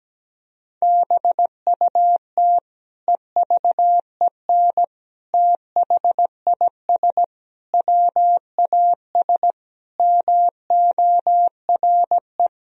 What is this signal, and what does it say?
Morse code, heard as a keyed tone, about 17 wpm, 700 Hz. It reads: BUT EVEN THIS WAS MORE
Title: Sentences from Top 1000 Words - Review Part 2 - 17wpm